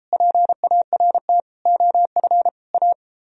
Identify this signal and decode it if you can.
Morse code, heard as a keyed tone, about 31 wpm, 700 Hz. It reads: PART OF A